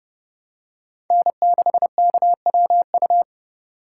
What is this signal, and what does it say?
Morse code, heard as a keyed tone, about 30 wpm, 700 Hz. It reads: N6KWU
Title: USA Call Signs - 5 Characters - 3x Character Spacing - 30wpm